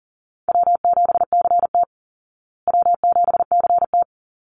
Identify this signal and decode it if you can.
Morse code, heard as a keyed tone, about 40 wpm, 700 Hz. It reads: W7CT W7CT
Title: Easy Call Signs Repeated 2x - 40wpm